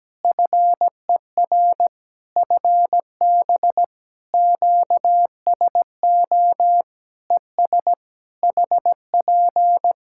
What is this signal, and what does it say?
Morse code, heard as a keyed tone, about 17 wpm, 700 Hz. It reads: FER FB QSO ES HP
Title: Practice QSO - Exchange 4 - Ending - 17wpm